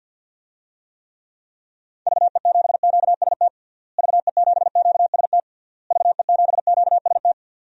Morse code, heard as a keyed tone, about 49 wpm, 700 Hz. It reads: VE7QST VE7QST VE7QST
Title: DXCC-100 Call Signs - 6 Characters - Repeated 3x - 50wpm